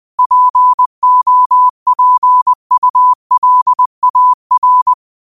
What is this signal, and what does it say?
Morse code, heard as a keyed tone, about 20 wpm, 1000 Hz. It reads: POPULAR
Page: New quotes every day in morse code at 20 Words per minute.